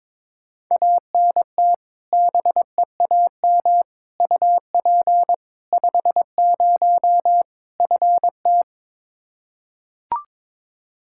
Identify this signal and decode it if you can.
Morse code, heard as a keyed tone, about 22 wpm, 700 Hz. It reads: ANT BEAM UP 50 FT
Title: Practice QSO - Exchange 2 - Info - 22wpm